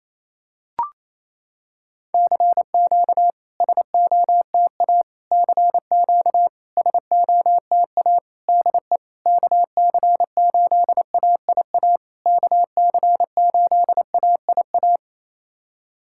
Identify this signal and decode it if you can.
Morse code, heard as a keyed tone, about 28 wpm, 690 Hz. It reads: CQ SOTA CQ SOTA DE KC8AIA KC8AIA
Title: SOTA QSO Practice 28wpm